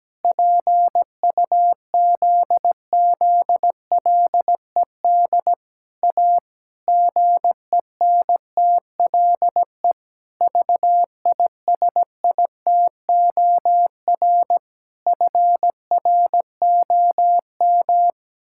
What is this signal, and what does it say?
Morse code, heard as a keyed tone, about 17 wpm, 700 Hz. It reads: PUZZLED A GENTLE VISITOR FROM
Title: Pangram Challenge v2 - 17wpm